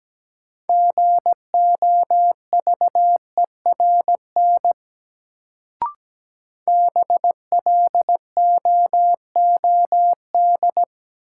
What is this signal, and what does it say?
Morse code, heard as a keyed tone, about 17 wpm, 700 Hz. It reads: GOVERN BLOOD
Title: Sentences from Top 1000 Words - Review Part 2 - New Word Drill - 17wpm